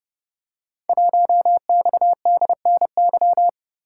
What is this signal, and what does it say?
Morse code, heard as a keyed tone, about 30 wpm, 700 Hz. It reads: 1XDNY
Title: Code Groups - 5 Letter:Numbers - 30wpm